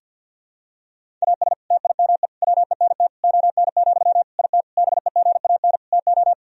Morse code, heard as a keyed tone, about 51 wpm, 700 Hz. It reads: AS TIME WENT ON, IT BEGAN TO
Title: Sentences from Top 1000 Words - Review Part 1 - 50wpm